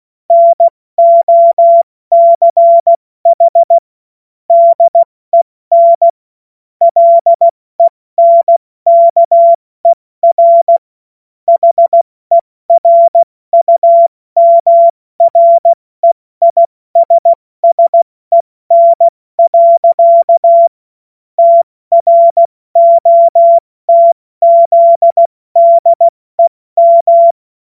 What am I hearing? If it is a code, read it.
Morse code, heard as a keyed tone, about 16 wpm, 680 Hz. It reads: NOCH DEN LENKER HERUMREISSEN. TROTZDEM
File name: Morsekurs / Witze / Witz 20. September 2022 70 F80